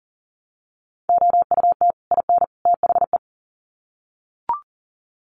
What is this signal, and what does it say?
Morse code, heard as a keyed tone, about 40 wpm, 700 Hz. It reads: OUT IN THE
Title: Sets of 3 Words 40wpm